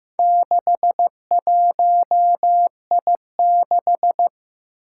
Morse code, heard as a keyed tone, about 15 wpm, 700 Hz. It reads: 61I6
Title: Four Challenging Letter-Number Combinations 15wpm